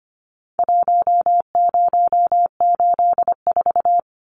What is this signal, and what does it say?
Morse code, heard as a keyed tone, about 25 wpm, 700 Hz. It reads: 1084
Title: Numbers - Contest Style - 4-Digit 25wpm